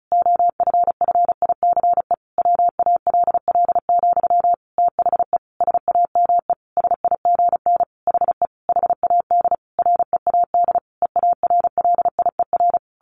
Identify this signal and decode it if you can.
Morse code, heard as a keyed tone, about 34 wpm, 700 Hz. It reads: OFFICE WALL, THE SAME SIGN HE HAD READ EARLIER
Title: A Sound Of Thunder 35wpm